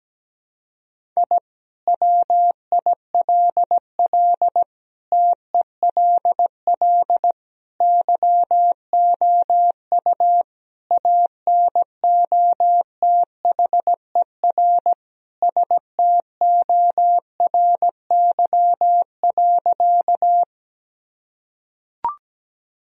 Morse code, heard as a keyed tone, about 17 wpm, 700 Hz. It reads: I WILL TELL YOU ANOTHER STORY.
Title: Sentences from Top 1000 Words - Review Part 1 - 17wpm